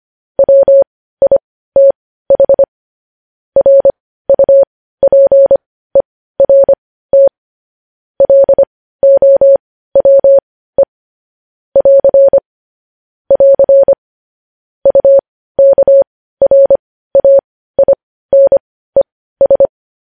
Morse code, heard as a keyed tone, about 25 wpm, spaced as for 15 wpm, 550 Hz. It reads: WITH RUPERT LOWE <AR> <AR> UKRAINES
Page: news headlines in morse code at 15 wpm